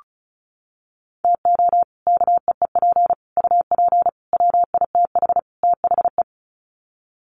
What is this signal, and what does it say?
Morse code, heard as a keyed tone, about 34 wpm, 700 Hz. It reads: TO KEEP UP WITH THE
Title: Sets of 5 Words Encore 35wpm